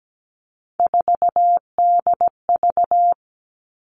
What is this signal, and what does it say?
Morse code, heard as a keyed tone, about 17 wpm, 700 Hz. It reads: 4DV
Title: Three Challenging Letter-Number Combinations 17wpm